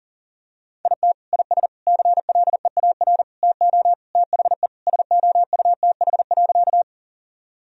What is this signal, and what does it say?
Morse code, heard as a keyed tone, about 40 wpm, 700 Hz. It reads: IT IS CLEAR TO THE SOUTH.
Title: Sentences from Top 1000 Words - Review Part 2 - 40wpm